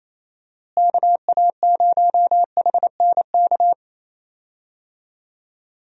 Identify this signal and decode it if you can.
Morse code, heard as a keyed tone, about 27 wpm, 700 Hz. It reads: KA0HNK
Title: USA Call Signs - 6 Characters - Repeated 3x - 28wpm